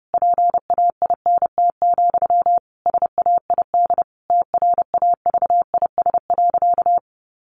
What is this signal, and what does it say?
Morse code, heard as a keyed tone, about 30 wpm, 700 Hz. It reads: PAINT, SAID TRAVIS.
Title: A Sound Of Thunder 30wpm